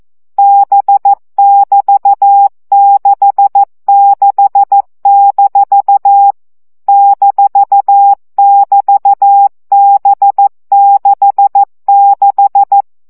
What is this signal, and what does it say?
Morse code, heard as a keyed tone, about 14 wpm, 800 Hz. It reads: B<BT>66- -<BT>B66
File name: CW gyakorlas b 6 egyenlo minusz 12 baud